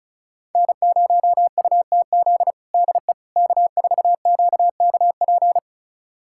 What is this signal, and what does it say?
Morse code, heard as a keyed tone, about 36 wpm, 700 Hz. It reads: N0UTZ DE K4QKP
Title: USA Callsign DE Callsign - 5 Characters - 35wpm